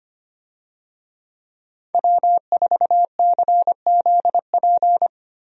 Morse code, heard as a keyed tone, about 25 wpm, 700 Hz. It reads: W4CZP
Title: USA Call Signs - 5 Characters - Repeated 3x - 25wpm